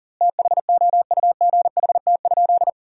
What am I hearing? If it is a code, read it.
Morse code, heard as a keyed tone, about 37 wpm, 700 Hz. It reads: THOUGHT?
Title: 2x Word-Spacing - Sentences from Top 300 Words - 40wpm